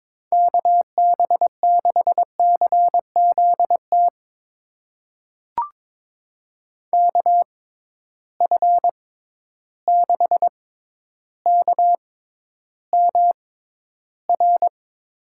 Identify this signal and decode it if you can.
Morse code, heard as a keyed tone, about 22 wpm, spaced as for 11 wpm, 700 Hz. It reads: KB6CZT KF6KMR
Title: USA Call Signs - 6 Characters - 6x Character Spacing - 22wpm